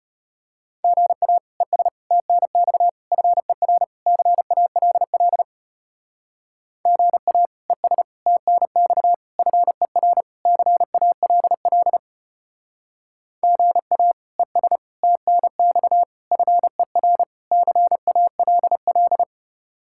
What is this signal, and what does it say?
Morse code, heard as a keyed tone, about 32 wpm, 700 Hz. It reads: GA ES TNX FER CALL GA ES TNX FER CALL GA ES TNX FER CALL